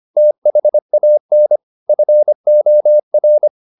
Morse code, heard as a keyed tone, about 25 wpm, 600 Hz. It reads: THAN FOR
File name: the old man and the sea part 1 25wpm